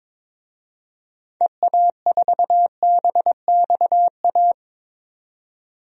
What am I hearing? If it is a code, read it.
Morse code, heard as a keyed tone, about 22 wpm, 700 Hz. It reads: EA4BXA
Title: DXCC-100 Call Signs - 6 Characters - 22wpm